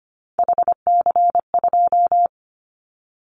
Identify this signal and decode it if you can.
Morse code, heard as a keyed tone, about 25 wpm, 700 Hz. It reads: HC2
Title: Callsign Prefixes for Top 100 DXCC - 25wpm